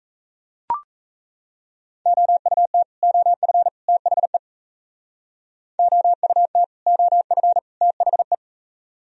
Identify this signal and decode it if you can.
Morse code, heard as a keyed tone, about 37 wpm, 700 Hz. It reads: OUT OF THE OUT OF THE